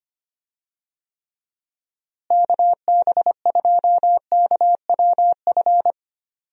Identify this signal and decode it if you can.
Morse code, heard as a keyed tone, about 25 wpm, 700 Hz. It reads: KB2KWF
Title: USA Call Signs - 6 Characters - Repeated 2x - 25wpm